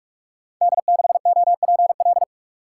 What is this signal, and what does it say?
Morse code, heard as a keyed tone, about 44 wpm, 700 Hz. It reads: D6OPL
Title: Code Groups - 5 Letter:Numbers - 45wpm